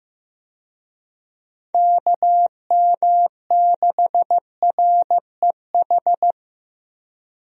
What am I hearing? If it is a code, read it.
Morse code, heard as a keyed tone, about 15 wpm, 700 Hz. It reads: KM6REH
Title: USA Call Signs - 6 Characters - Repeated 2x - 15wpm